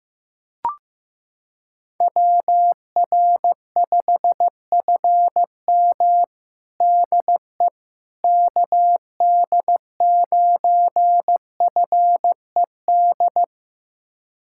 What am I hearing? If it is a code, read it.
Morse code, heard as a keyed tone, about 15 wpm, 700 Hz. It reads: WR5FM DE KD9FED
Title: USA Callsign DE Callsign 15wpm